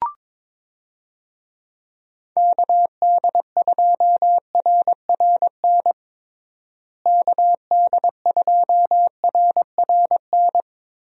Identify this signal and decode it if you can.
Morse code, heard as a keyed tone, about 22 wpm, 700 Hz. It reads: KD2RRN KD2RRN
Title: USA Call Signs - 6 Characters - Repeated 3x - 22wpm